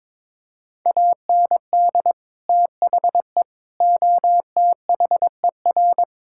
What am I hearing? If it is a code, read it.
Morse code, heard as a keyed tone, about 22 wpm, 700 Hz. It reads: AND THE OTHER
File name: practice / 22wpm / Sentences with 3-Word Phrases - 22wpm